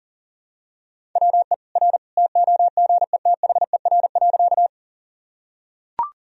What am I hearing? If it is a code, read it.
Morse code, heard as a keyed tone, about 40 wpm, 700 Hz. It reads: WE R TOGETHER.
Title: Sentences from Top 300 Words and Abbreviations 40wpm